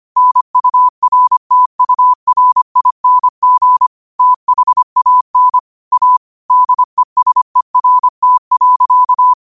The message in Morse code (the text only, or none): NURTURING THAN A DESERT.